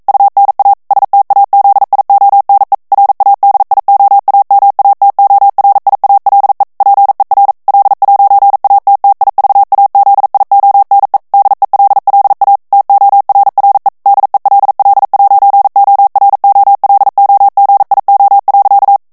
Morse code, heard as a keyed tone, about 42 wpm, 800 Hz. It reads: UNA STAZIONE RADIOAMATORIALE PER L'ATTIVAZIONE DELLA TORRE DELL'OROLOGIO.
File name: ITALIEN texte clair 1 vitesse 40 mots minute